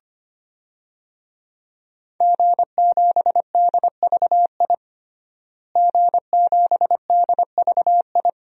Morse code, heard as a keyed tone, about 25 wpm, 700 Hz. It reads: G7DVI G7DVI
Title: DXCC-100 Call Signs - 5 Characters - Repeated 3x - 25wpm